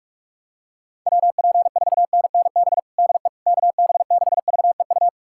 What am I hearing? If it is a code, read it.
Morse code, heard as a keyed tone, about 42 wpm, 690 Hz. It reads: WP4NNB DE KB6VEU
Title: USA Callsign DE Callsign - 6 Characters - 45wpm